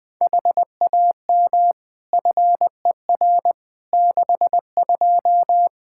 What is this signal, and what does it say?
Morse code, heard as a keyed tone, about 20 wpm, 700 Hz. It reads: HAM FER 62
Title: Practice QSO - Exchange 3 - More Info - 20wpm